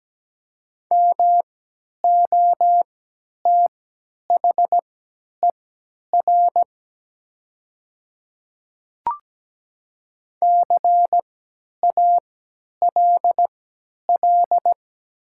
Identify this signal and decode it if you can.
Morse code, heard as a keyed tone, about 17 wpm, spaced as for 10 wpm, 700 Hz. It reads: MOTHER CALL
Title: 3x Character Spacing - Top 200 Words - 17wpm